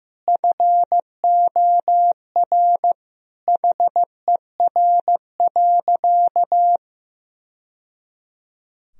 Morse code, heard as a keyed tone, about 15 wpm, 700 Hz. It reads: FOR HER.
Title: Sentences from Top 1000 Words - Review Part 3 - 15wpm